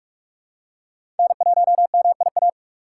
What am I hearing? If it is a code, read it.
Morse code, heard as a keyed tone, about 46 wpm, 700 Hz. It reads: N1MIA
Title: USA Call Signs - 2x Character Spacing - 45wpm